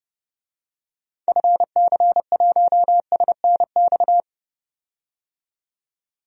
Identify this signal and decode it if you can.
Morse code, heard as a keyed tone, about 30 wpm, 700 Hz. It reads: FC1SNX
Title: DXCC-100 Call Signs - 6 Characters - 30wpm